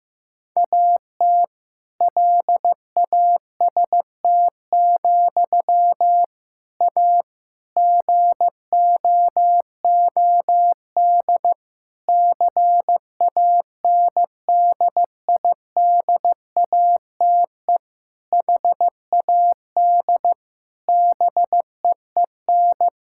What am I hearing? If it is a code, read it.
Morse code, heard as a keyed tone, about 15 wpm, 700 Hz. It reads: AT LAST, A GOOD CANDIDATE HAD BEEN
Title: Sentences from Top 1600 Words 15wpm